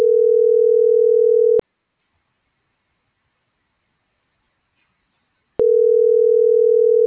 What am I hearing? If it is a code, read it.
Ringback tone (North America), 440 + 480 Hz.